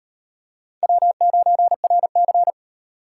Morse code, heard as a keyed tone, about 37 wpm, 700 Hz. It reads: W9RC